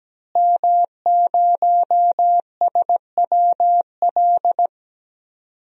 Morse code, heard as a keyed tone, about 17 wpm, 700 Hz. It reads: M0SWL
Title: DXCC-100 Call Signs - 5 Characters - Repeated 3x - 17wpm